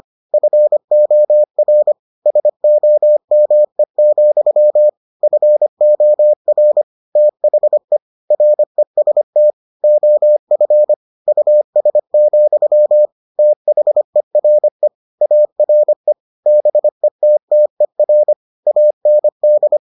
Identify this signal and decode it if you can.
Morse code, heard as a keyed tone, about 25 wpm, 600 Hz. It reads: FOR SOME, FOR THE REST OF US, THERE ARE BETTER AND